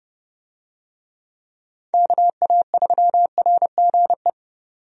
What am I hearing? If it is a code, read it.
Morse code, heard as a keyed tone, about 31 wpm, 700 Hz. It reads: KA3RGE
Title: USA Call Signs - 6 Characters - 30wpm